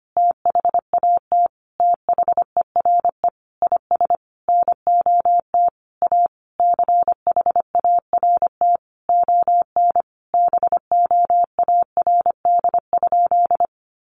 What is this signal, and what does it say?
Morse code, heard as a keyed tone, about 25 wpm, 700 Hz. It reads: THAT THERE IS NOT A CHART ON BOARD?
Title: Sentences from Top 1300 Words 25wpm